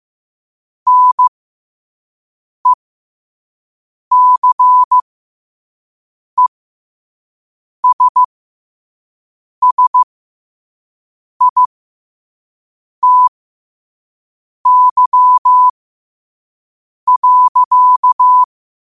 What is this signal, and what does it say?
Morse code, heard as a keyed tone, about 15 wpm, spaced as for 5 wpm, 1000 Hz. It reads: NECESSITY.
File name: qotdfiles / 5wpm